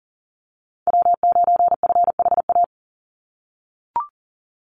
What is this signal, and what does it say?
Morse code, heard as a keyed tone, about 40 wpm, 700 Hz. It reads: W9FHA
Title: NA SOTA Chaser Callsigns 40wpm